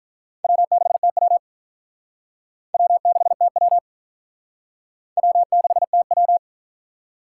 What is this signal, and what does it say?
Morse code, heard as a keyed tone, about 53 wpm, 700 Hz. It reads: W6TW W6TW W6TW